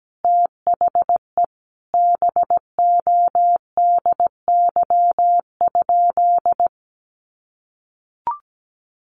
Morse code, heard as a keyed tone, about 17 wpm, 700 Hz. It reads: THE BODY?
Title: Sentences from Top 400 Words 17wpm